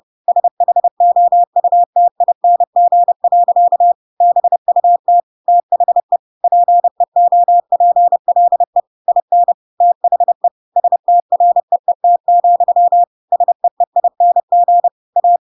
Morse code, heard as a keyed tone, about 30 wpm, 700 Hz. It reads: SHOUTING. BUT THE PEOPLE IN THE STREET, SEEING A
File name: Pinocchio / Pinocchio ch3 30 30 ik1hge